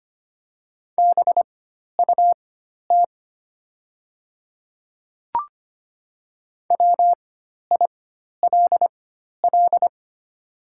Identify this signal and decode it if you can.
Morse code, heard as a keyed tone, about 25 wpm, spaced as for 12 wpm, 690 Hz. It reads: BUT WILL
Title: 4x Character Spacing - Top 100 Words - 25wpm